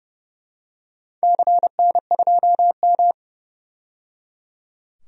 Morse code, heard as a keyed tone, about 30 wpm, 700 Hz. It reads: CN2M
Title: DXCC-100 Call Signs - 4 Characters - 30wpm